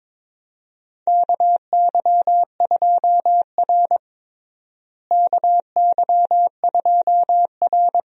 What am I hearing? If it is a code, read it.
Morse code, heard as a keyed tone, about 22 wpm, 700 Hz. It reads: KY2R KY2R
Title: USA Call Signs - 4 Characters - Repeated 3x - 22wpm